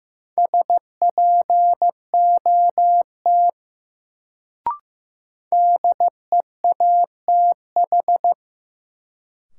Morse code, heard as a keyed tone, about 15 wpm, 700 Hz. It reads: SPOT DEATH
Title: Sentences from Top 1100 Words - New Word Drill Alt - 15wpm